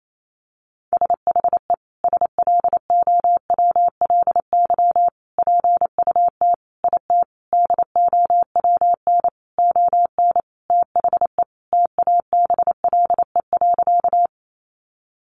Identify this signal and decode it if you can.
Morse code, heard as a keyed tone, about 28 wpm, 700 Hz. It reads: SHE SLOWLY PUT IT DOWN ON THE TABLE.
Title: Sentences from Top 1100 Words 28wpm